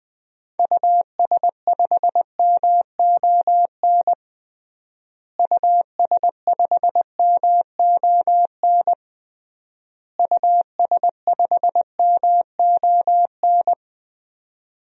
Morse code, heard as a keyed tone, about 20 wpm, 700 Hz. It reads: US5MON US5MON US5MON
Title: DXCC-100 Call Signs - 6 Characters - Repeated 3x - 20wpm